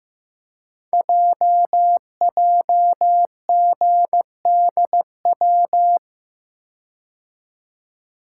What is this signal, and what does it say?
Morse code, heard as a keyed tone, about 15 wpm, 690 Hz. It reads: JJGDW
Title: Code Groups - 5 Letters - 15wpm